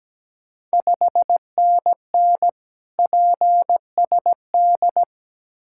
Morse code, heard as a keyed tone, about 17 wpm, 700 Hz. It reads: 5NN PSD
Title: Ontario QSO Party Exchange 17wpm